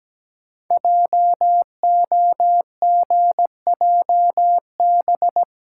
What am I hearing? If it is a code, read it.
Morse code, heard as a keyed tone, about 17 wpm, 700 Hz. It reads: JOGJB
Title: Code Groups - 5 Letters - 17wpm